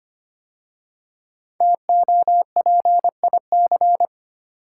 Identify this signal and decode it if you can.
Morse code, heard as a keyed tone, about 25 wpm, 700 Hz. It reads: TOPIC
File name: 5 Letter Words - First 3 Letters Given - 25wpm